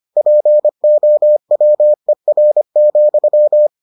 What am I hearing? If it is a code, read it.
Morse code, heard as a keyed tone, about 25 wpm, 600 Hz. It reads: POWER,